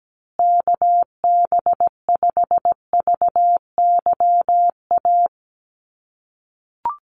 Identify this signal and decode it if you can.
Morse code, heard as a keyed tone, about 17 wpm, 700 Hz. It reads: KB5VYA
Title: USA Call Signs Repeated 2x 17wpm